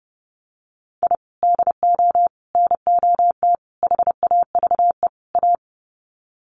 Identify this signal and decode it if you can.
Morse code, heard as a keyed tone, about 30 wpm, 700 Hz. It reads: I DO NOT HAVE A
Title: Sets of 5 Words 30wpm